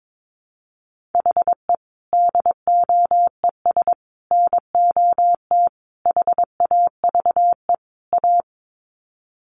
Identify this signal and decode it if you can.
Morse code, heard as a keyed tone, about 22 wpm, 700 Hz. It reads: HE DOES NOT HAVE A
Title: Sets of 5 Words Encore 22wpm